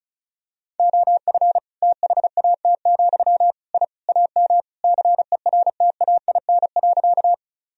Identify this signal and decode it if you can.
Morse code, heard as a keyed tone, about 34 wpm, 700 Hz. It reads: OF THAT, I AM CERTAIN.
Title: Sentences from Top 1000 Words - Review Part 2 - 35wpm